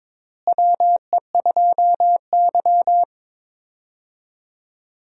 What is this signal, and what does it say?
Morse code, heard as a keyed tone, about 22 wpm, 700 Hz. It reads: WE2Y
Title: USA Call Signs - 4 Characters - Repeated 2x - 22wpm